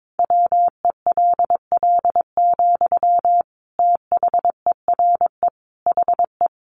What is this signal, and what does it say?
Morse code, heard as a keyed tone, about 22 wpm, 700 Hz. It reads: WELL, THERE HE
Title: Sentences from Top 500 Words 22wpm